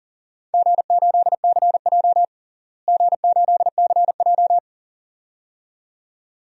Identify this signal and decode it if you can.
Morse code, heard as a keyed tone, about 40 wpm, 700 Hz. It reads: G8CJ G8CJ
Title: DXCC-100 Call Signs - 4 Characters - Repeated 2x - 40wpm